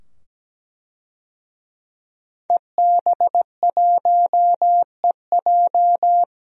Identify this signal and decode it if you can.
Morse code, heard as a keyed tone, about 17 wpm, 700 Hz. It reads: EB1EJ
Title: DXCC-100 Call Signs - 5 Characters - 17wpm